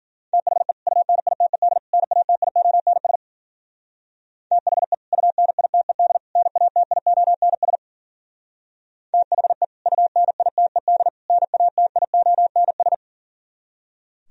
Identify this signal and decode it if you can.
Morse code, heard as a keyed tone, about 48 wpm, 700 Hz. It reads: THE UNITED NATIONS THE UNITED NATIONS THE UNITED NATIONS
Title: Speed-Racing - Sets of 3 Words 40wpm